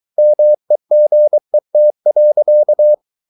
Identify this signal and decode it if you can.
Morse code, heard as a keyed tone, about 23 wpm, 600 Hz. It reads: MEGET.
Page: Ferie 23 wpm